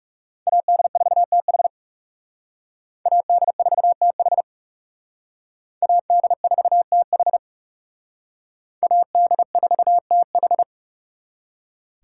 Morse code, heard as a keyed tone, about 36 wpm, 700 Hz. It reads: AD4TH AD4TH AD4TH AD4TH
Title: Speed Racing - USA 2x2 Call Signs 30wpm